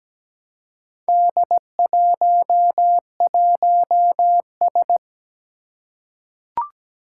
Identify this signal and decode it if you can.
Morse code, heard as a keyed tone, about 17 wpm, 700 Hz. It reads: D11S
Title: Four Challenging Letter-Number Combinations 17wpm